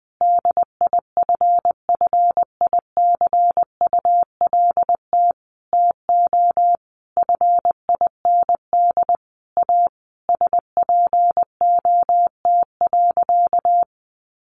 Morse code, heard as a keyed tone, about 20 wpm, 700 Hz. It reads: DIFFICULT TO FIND A SPOT.
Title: Sentences from Top 1100 Words 20wpm